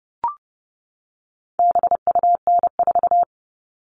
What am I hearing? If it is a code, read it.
Morse code, heard as a keyed tone, about 30 wpm, 700 Hz. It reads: BUN4
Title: All Cut Numbers - 4-Digit 30wpm